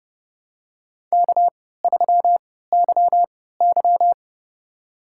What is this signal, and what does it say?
Morse code, heard as a keyed tone, about 30 wpm, spaced as for 17 wpm, 700 Hz. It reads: K3YY
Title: USA Call Signs - 4 Characters - 3x Character Spacing - 30wpm